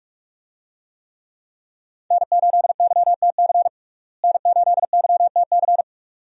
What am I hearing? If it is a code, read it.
Morse code, heard as a keyed tone, about 46 wpm, 700 Hz. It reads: N8YTC N8YTC
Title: USA Call Signs - 5 Characters - Repeated 2x - 45wpm